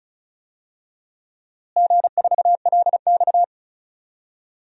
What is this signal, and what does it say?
Morse code, heard as a keyed tone, about 36 wpm, 700 Hz. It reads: G4LX
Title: DXCC-100 Call Signs - 4 Characters - Repeated 3x - 35wpm